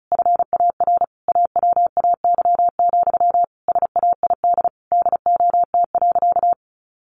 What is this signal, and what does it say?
Morse code, heard as a keyed tone, about 34 wpm, 700 Hz. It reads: FAR AWAY, SAID DOT.
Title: Sentences from Top 1500 Words 35wpm